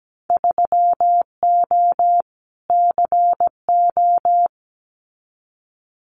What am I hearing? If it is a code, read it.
Morse code, heard as a keyed tone, about 17 wpm, 700 Hz. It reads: 3O CO
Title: Winter Field Day Exchanges Encore 17wpm